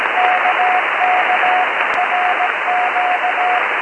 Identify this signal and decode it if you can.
Morse code, heard as a keyed tone, about 17 wpm, 730 Hz, with radio noise. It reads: KK RQ